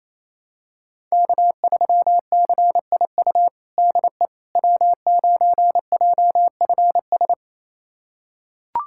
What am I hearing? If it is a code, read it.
Morse code, heard as a keyed tone, about 28 wpm, 700 Hz. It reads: K3CIU DE W9JFS
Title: USA Callsign DE Callsign - 5 Characters - 28wpm